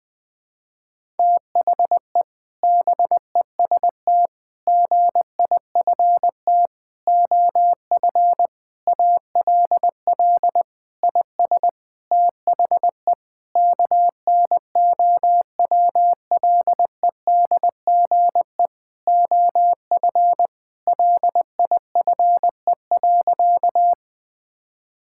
Morse code, heard as a keyed tone, about 20 wpm, 700 Hz. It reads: THE BEST GIFT OF ALL IS THE KNOWLEDGE OF LIFE.
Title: Sentences from Top 1000 Words - Review Part 6 - 20wpm